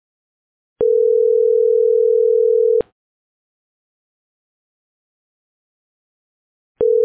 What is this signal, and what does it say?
Ringback tone (North America), 440 + 480 Hz.